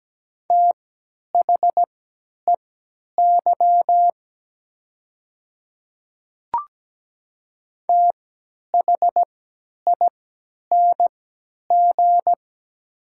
Morse code, heard as a keyed tone, about 17 wpm, spaced as for 10 wpm, 700 Hz. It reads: THEY THING
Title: 3x Character Spacing - Top 100 Words - 17wpm